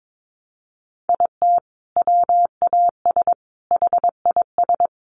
Morse code, heard as a keyed tone, about 22 wpm, 700 Hz. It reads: IT WAS HIS
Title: Sentences from Top 1200 Words 22wpm